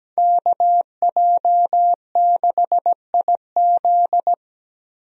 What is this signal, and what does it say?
Morse code, heard as a keyed tone, about 17 wpm, 700 Hz. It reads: KJ6IZ
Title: USA Call Signs - 5 Characters - 17wpm